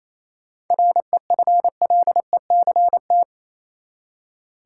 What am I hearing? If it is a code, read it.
Morse code, heard as a keyed tone, about 28 wpm, 700 Hz. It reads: REFLECT
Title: Seven Letter Words - 28wpm